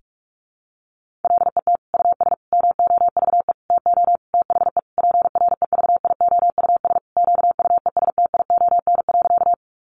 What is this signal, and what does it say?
Morse code, heard as a keyed tone, about 44 wpm, 700 Hz. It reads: LET US MOVE TO THE PREVIOUS QUESTION.
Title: Sentences from Top 1500 Words 45wpm